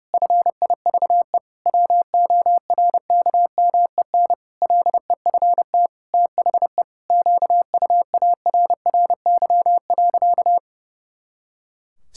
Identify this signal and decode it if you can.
Morse code, heard as a keyed tone, about 32 wpm, 700 Hz. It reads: FIVE WORKMEN LEFT THE QUARRY.
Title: Pangram Challenge v2 - 30wpm